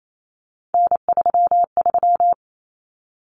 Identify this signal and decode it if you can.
Morse code, heard as a keyed tone, about 28 wpm, 700 Hz. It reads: N33
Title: Common Cut Numbers - 3-Digit 28wpm